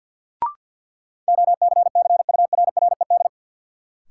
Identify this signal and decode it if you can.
Morse code, heard as a keyed tone, about 53 wpm, 700 Hz. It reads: OCCURRED
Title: Sets of 5 Words Encore - New Word Drill - 50wpm